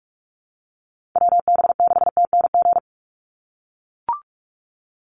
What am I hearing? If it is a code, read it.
Morse code, heard as a keyed tone, about 44 wpm, 700 Hz. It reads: WB6TNG
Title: USA Call Signs 45wpm